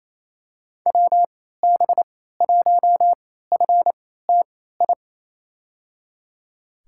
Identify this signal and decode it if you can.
Morse code, heard as a keyed tone, about 28 wpm, spaced as for 16 wpm, 700 Hz. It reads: WB1FTI
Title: USA Call Signs - 6 Characters - 3x Character Spacing - 28wpm